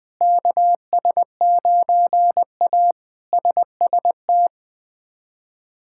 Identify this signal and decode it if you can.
Morse code, heard as a keyed tone, about 20 wpm, 700 Hz. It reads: KS9A SST
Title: K1USN SST QSO Practice - 20wpm